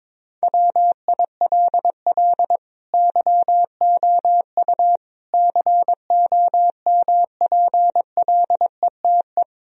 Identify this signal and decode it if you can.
Morse code, heard as a keyed tone, about 22 wpm, 700 Hz. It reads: WILL YOU COMPLETE
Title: Sentences from Top 1000 Words - Review Part 2 - 22wpm